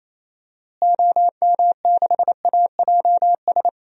Morse code, heard as a keyed tone, about 28 wpm, 700 Hz. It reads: OM6AJS
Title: DXCC-100 Call Signs - 6 Characters - Repeated 2x - 28wpm